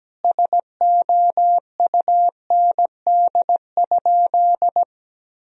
Morse code, heard as a keyed tone, about 17 wpm, 700 Hz. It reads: SOUND?
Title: Sentences from Top 1000 Words - Review Part 1 - 17wpm